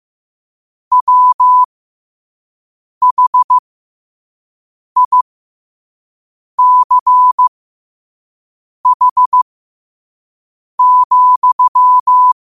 Morse code, heard as a keyed tone, about 15 wpm, spaced as for 5 wpm, 1000 Hz. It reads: WHICH,
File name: qotdfiles / 5wpm